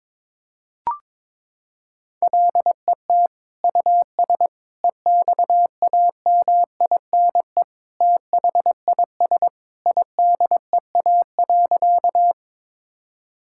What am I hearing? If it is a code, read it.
Morse code, heard as a keyed tone, about 22 wpm, 700 Hz. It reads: LET US EXAMINE THIS IDEA.
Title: Sentences from Top 1600 Words 22wpm